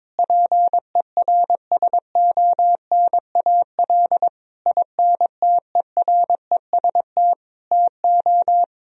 Morse code, heard as a keyed tone, about 22 wpm, 700 Hz. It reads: PERSONAL INTEREST TO
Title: Sentences from Top 1000 Words - Review Part 4 - 22wpm